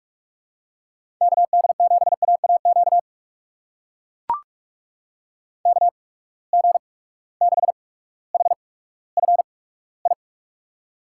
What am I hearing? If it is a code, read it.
Morse code, heard as a keyed tone, about 44 wpm, spaced as for 19 wpm, 700 Hz. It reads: KD7AAQ KG6HFI
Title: USA Call Signs - 6 Characters - 8x Character Spacing - 45wpm